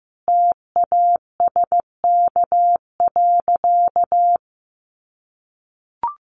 Morse code, heard as a keyed tone, about 15 wpm, 700 Hz. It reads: TASK.
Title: Sentences from Top 1400 Words 15wpm